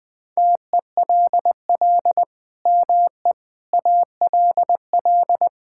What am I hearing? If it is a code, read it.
Morse code, heard as a keyed tone, about 20 wpm, 700 Hz. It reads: TELL ME ALL
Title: Sentences from Top 1000 Words - Review Part 6 - 20wpm